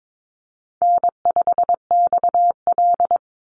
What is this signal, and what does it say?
Morse code, heard as a keyed tone, about 22 wpm, 700 Hz. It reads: N5XL
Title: NA SOTA Chaser Callsigns 22wpm